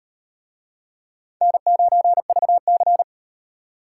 Morse code, heard as a keyed tone, about 36 wpm, 700 Hz. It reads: N9VC